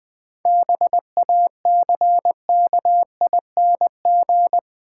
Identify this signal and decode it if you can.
Morse code, heard as a keyed tone, about 20 wpm, 700 Hz. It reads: BACKING
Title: 7 Letter Words - First 3 Letters Given - 20wpm